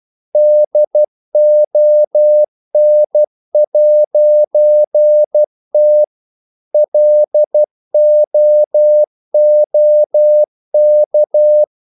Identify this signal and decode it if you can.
Morse code, heard as a keyed tone, about 12 wpm, 600 Hz. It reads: DON'T LOOK
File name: CW / Practice / If12